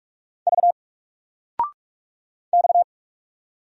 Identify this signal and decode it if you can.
Morse code, heard as a keyed tone, about 44 wpm, 700 Hz. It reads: V X
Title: Single Letter-Number Alt 45wpm